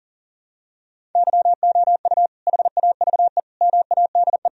Morse code, heard as a keyed tone, about 40 wpm, 700 Hz. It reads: YOU HAVE MADE
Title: Sentences from Top 1000 Words - Review Part 1 - 40wpm